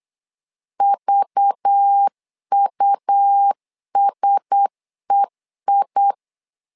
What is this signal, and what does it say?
Morse code, heard as a keyed tone, about 8 wpm, 800 Hz. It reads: VUSEI
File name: LEZIONI / 5 LBVC / 5 LBVC 8wpm 2935